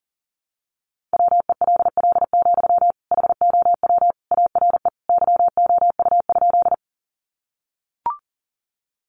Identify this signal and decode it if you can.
Morse code, heard as a keyed tone, about 40 wpm, 700 Hz. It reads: WELL, HOW ARE YOU?
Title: Sentences from Top 200 Words 40wpm